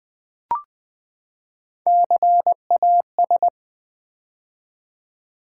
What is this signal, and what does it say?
Morse code, heard as a keyed tone, about 20 wpm, 700 Hz. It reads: CAS
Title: Build A Word - 3 Letters At A Time - 6-Letter Words - 20wpm